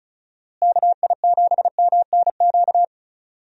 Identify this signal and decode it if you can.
Morse code, heard as a keyed tone, about 36 wpm, 700 Hz. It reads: KI7MNQ
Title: USA Call Signs - 6 Characters - Repeated 2x - 35wpm